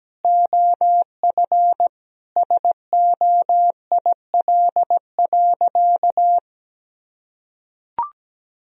Morse code, heard as a keyed tone, about 17 wpm, 700 Hz. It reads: OF SOIL.
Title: Sentences from Top 1000 Words - Review Part 4 - 17wpm